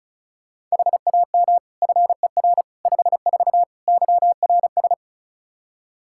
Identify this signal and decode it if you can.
Morse code, heard as a keyed tone, about 34 wpm, 700 Hz. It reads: HAM FER 54 YRS
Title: K1USN SST QSO Practice 35wpm